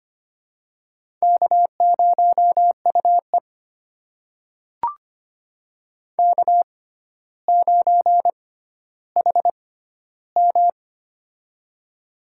Morse code, heard as a keyed tone, about 25 wpm, spaced as for 13 wpm, 700 Hz. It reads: K0UE K9HM
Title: USA Call Signs - 4 Characters - 6x Character Spacing - 25wpm